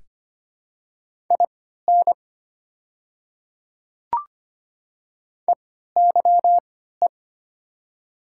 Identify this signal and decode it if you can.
Morse code, heard as a keyed tone, about 25 wpm, spaced as for 14 wpm, 690 Hz. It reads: IN EYE
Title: 3x Character Spacing - Top 300 Words - 25wpm